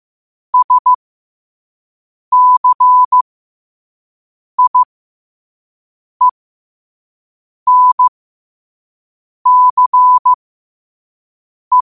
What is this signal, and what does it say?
Morse code, heard as a keyed tone, about 15 wpm, spaced as for 5 wpm, 1000 Hz. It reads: SCIENCE
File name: qotdfiles / 5wpm